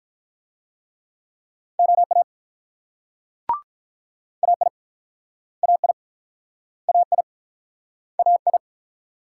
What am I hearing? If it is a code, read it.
Morse code, heard as a keyed tone, about 48 wpm, 700 Hz. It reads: OA AI AI AI AI